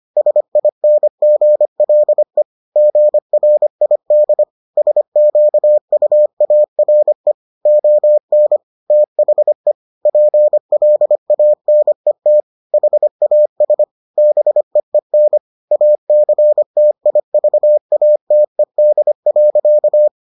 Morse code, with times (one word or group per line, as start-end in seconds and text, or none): 0.2-2.4 SINGLE
2.8-4.4 GRID
4.8-7.3 SQUARE
7.7-8.6 ON
8.9-9.7 THE
10.1-12.4 PLANET
12.7-13.8 HAS
14.2-15.4 BEEN
15.7-20.1 ACTIVATED.